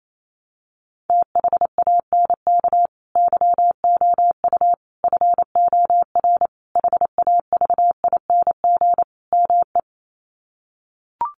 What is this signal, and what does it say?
Morse code, heard as a keyed tone, about 28 wpm, 700 Hz. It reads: THANK YOU FOR HAVING ME
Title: Sets of 5 Words Encore 28wpm